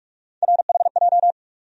Morse code, heard as a keyed tone, about 47 wpm, 700 Hz. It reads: RHJ
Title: USA Call Sign Postfixes - 3 Letters 45wpm